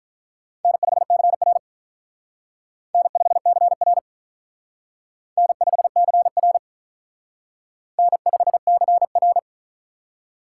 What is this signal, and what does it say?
Morse code, heard as a keyed tone, about 45 wpm, 700 Hz. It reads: N5CR N5CR N5CR N5CR